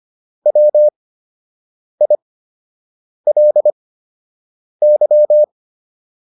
Morse code, heard as a keyed tone, about 25 wpm, spaced as for 7 wpm, 600 Hz. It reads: WILY